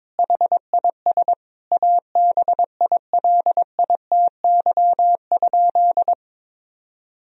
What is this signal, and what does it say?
Morse code, heard as a keyed tone, about 22 wpm, 700 Hz. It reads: HIS ABILITY?
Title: Sentences from Top 1000 Words - Review Part 4 - 22wpm